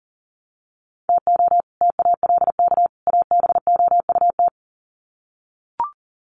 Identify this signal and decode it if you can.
Morse code, heard as a keyed tone, about 40 wpm, 700 Hz. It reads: TO TALK ABOUT
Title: Sets of 3 Words 40wpm